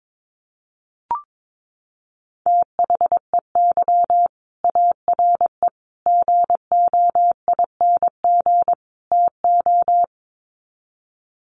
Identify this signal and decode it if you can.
Morse code, heard as a keyed tone, about 22 wpm, 700 Hz. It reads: THEY ARE GOING TO
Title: Sets of 4 Words 22wpm